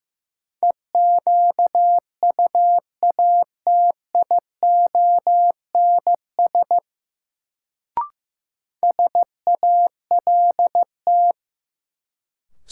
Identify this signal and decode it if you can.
Morse code, heard as a keyed tone, about 16 wpm, 700 Hz. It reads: EQUATIONS SALT
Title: Sets of 5 Words - New Word Drill - 15wpm